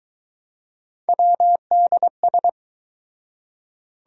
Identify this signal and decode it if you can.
Morse code, heard as a keyed tone, about 24 wpm, 700 Hz. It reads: WDS